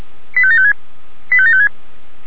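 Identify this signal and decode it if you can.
Five-tone selective call (ZVEI), heard as quick runs of five tones, tones of about 70 ms. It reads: 85756 85756
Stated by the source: Five-tone ZVEI selective call 85726 received by SDR.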